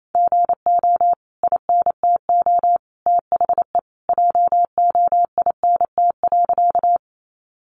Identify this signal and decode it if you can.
Morse code, heard as a keyed tone, about 28 wpm, 700 Hz. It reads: GO INTO THE JOINT.
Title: Sentences from Top 1300 Words 28wpm